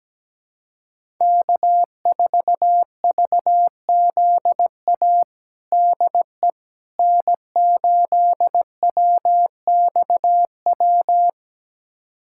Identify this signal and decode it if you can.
Morse code, heard as a keyed tone, about 17 wpm, 700 Hz. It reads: K4VZA DE N8WXW
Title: USA Callsign DE Callsign - 5 Characters - 17wpm